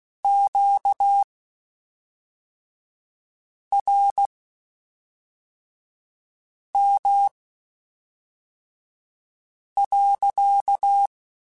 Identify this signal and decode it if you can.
Morse code, heard as a keyed tone, about 16 wpm, spaced as for 3 wpm, 800 Hz. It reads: QRM.
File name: Q Calls / Q Calls-mp3s / Q Calls sppc @5